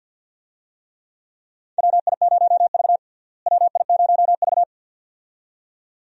Morse code, heard as a keyed tone, about 46 wpm, 700 Hz. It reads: WI0V WI0V
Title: USA Call Signs - 4 Characters - Repeated 2x - 50wpm